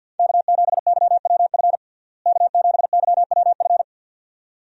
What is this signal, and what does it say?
Morse code, heard as a keyed tone, about 50 wpm, 700 Hz. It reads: K7YWF K7YWF
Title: USA Call Signs - 5 Characters - Repeated 2x - 50wpm